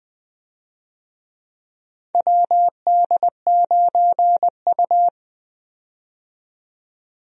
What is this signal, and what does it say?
Morse code, heard as a keyed tone, about 20 wpm, 700 Hz. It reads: WD9U
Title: USA Call Signs - 4 Characters - 20wpm